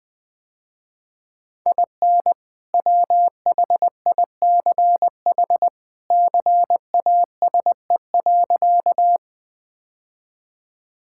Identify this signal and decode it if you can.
Morse code, heard as a keyed tone, about 20 wpm, 700 Hz. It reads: IN WHICH CASE.
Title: Sentences with 2-Word Phrases - Spoken Beginning - 20wpm